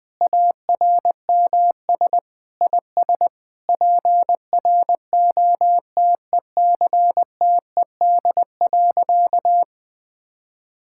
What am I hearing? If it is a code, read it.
Morse code, heard as a keyed tone, about 20 wpm, 700 Hz. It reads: ARMS IS PROTECTED.
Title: Sentences with 9-Word Phrases - Spoken Beginning - 20wpm